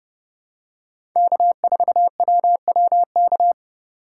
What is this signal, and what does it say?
Morse code, heard as a keyed tone, about 32 wpm, 700 Hz. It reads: K4WWK
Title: USA Call Signs - 5 Characters - 4x Character Spacing - 30wpm